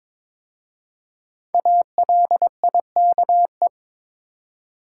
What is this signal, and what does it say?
Morse code, heard as a keyed tone, about 22 wpm, 700 Hz. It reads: ALIKE
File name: 5 Letter Words - First 3 Letters Given - 22wpm